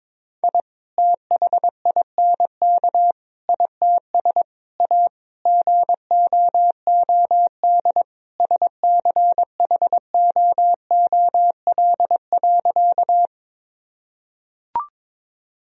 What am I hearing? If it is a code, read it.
Morse code, heard as a keyed tone, about 22 wpm, 700 Hz. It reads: I THINK ITS A GOOD SCHOOL.
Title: Sentences from Top 1000 Words - Review Part 1 - 22wpm